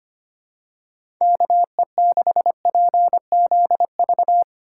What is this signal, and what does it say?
Morse code, heard as a keyed tone, about 25 wpm, 700 Hz. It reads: KE6PZV
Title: USA Call Signs - 6 Characters - 4x Character Spacing - 25wpm